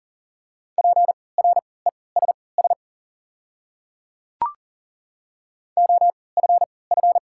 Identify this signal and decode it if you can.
Morse code, heard as a keyed tone, about 40 wpm, spaced as for 23 wpm, 700 Hz. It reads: PRESS OFF
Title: 3x Character Spacing - Top 300 Words - 40wpm